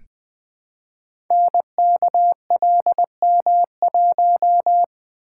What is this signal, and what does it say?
Morse code, heard as a keyed tone, about 20 wpm, 700 Hz. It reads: NKLM1
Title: Code Groups - 5 Letter:Numbers - 20wpm